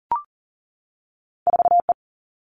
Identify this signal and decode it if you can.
Morse code, heard as a keyed tone, about 38 wpm, 700 Hz. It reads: 4E
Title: All Cut Numbers - 2-Digit 40wpm